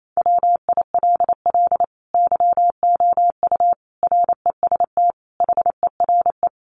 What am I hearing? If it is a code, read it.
Morse code, heard as a keyed tone, about 28 wpm, 700 Hz. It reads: WILL YOU REST HERE
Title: Sentences from Top 500 Words 28wpm